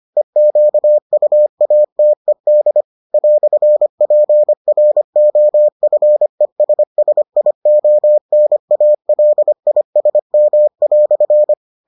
Morse code, heard as a keyed tone, about 25 wpm, 600 Hz. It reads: EQUATED "PROFESSIONALISM"